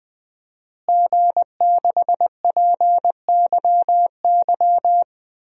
Morse code, heard as a keyed tone, about 20 wpm, 700 Hz. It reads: G6PYY
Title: DXCC-100 Call Signs - 5 Characters - Repeated 3x - 20wpm